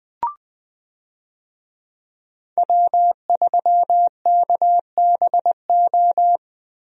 Morse code, heard as a keyed tone, about 20 wpm, 700 Hz. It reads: W3KBO
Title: USA Call Signs - 5 Characters - Repeated 3x - 20wpm